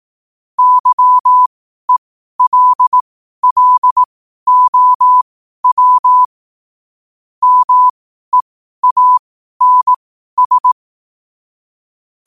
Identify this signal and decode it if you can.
Morse code, heard as a keyed tone, about 18 wpm, spaced as for 12 wpm, 1000 Hz. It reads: YELLOW MEANS
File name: qotdfiles / 12wpm